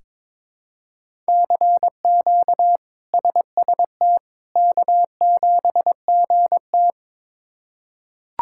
Morse code, heard as a keyed tone, about 21 wpm, 700 Hz. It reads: CQ SST K7GT
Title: K1USN SST QSO Practice - 22wpm